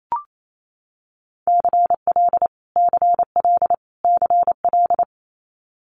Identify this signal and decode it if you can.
Morse code, heard as a keyed tone, about 28 wpm, 700 Hz. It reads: CL CL CL
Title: Multi-letter Phonemes Repeated 3x 28wpm